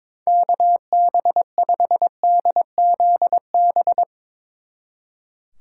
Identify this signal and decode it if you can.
Morse code, heard as a keyed tone, about 22 wpm, 700 Hz. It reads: KB5DZB
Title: Speed Racing - USA 2x3 Call Signs 22wpm